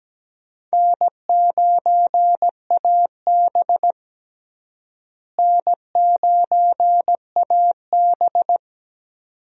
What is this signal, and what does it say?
Morse code, heard as a keyed tone, about 17 wpm, 700 Hz. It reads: N9AB N9AB
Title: USA Call Signs - 4 Characters - Repeated 2x - 17wpm